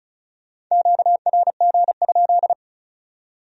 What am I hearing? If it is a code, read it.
Morse code, heard as a keyed tone, about 44 wpm, 700 Hz. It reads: QRG?
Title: Speed-Racing Q-Codes - 35wpm